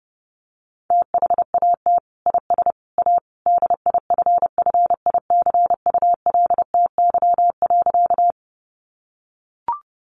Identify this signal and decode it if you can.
Morse code, heard as a keyed tone, about 30 wpm, 700 Hz. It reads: THAT IS A DIFFICULTY.
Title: Sentences from Top 1500 Words 30wpm